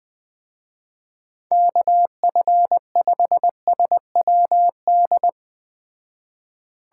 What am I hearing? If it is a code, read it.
Morse code, heard as a keyed tone, about 20 wpm, 700 Hz. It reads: KF5SWD